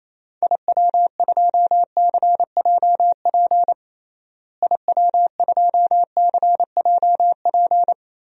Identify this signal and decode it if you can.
Morse code, heard as a keyed tone, about 28 wpm, 700 Hz. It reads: IW2CJP IW2CJP
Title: DXCC-100 Call Signs - 6 Characters - Repeated 3x - 28wpm